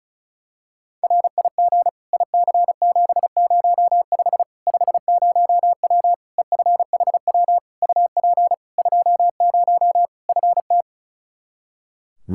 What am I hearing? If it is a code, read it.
Morse code, heard as a keyed tone, about 34 wpm, 700 Hz. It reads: RIG IC705 50W EFHW UP 20 FT
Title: Practice QSO - Exchange 2 - Info - 35wpm